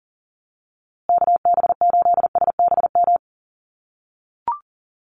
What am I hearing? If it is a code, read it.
Morse code, heard as a keyed tone, about 40 wpm, 700 Hz. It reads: KB8SBM
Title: USA Call Signs Repeated 2x 40wpm